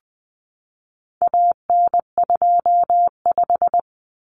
Morse code, heard as a keyed tone, about 20 wpm, 700 Hz. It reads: AN25
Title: Common Cut Numbers - 4-Digit 20wpm